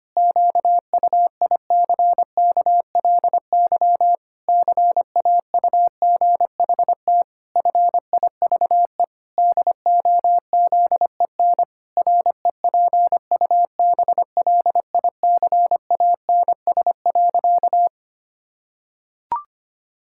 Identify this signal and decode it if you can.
Morse code, heard as a keyed tone, about 25 wpm, 700 Hz. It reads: QUICKLY CAUGHT FIVE DOZEN REPUBLICANS.
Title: Pangram Challenge v2 - 25wpm